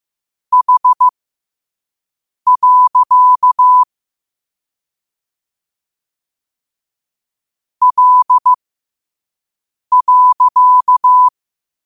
Morse code, heard as a keyed tone, about 15 wpm, spaced as for 5 wpm, 1000 Hz. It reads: H. L.
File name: 5wpm